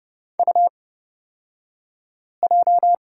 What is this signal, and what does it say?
Morse code, heard as a keyed tone, about 31 wpm, spaced as for 5 wpm, 700 Hz. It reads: UJ
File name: Lesson 38d - Letter q - Two Characters 30wpm